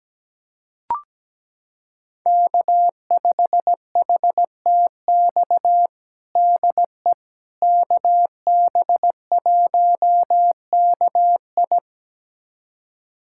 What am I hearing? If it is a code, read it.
Morse code, heard as a keyed tone, about 17 wpm, 700 Hz. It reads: K5HTX DE KB1KI
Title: USA Callsign DE Callsign - 5 Characters - 17wpm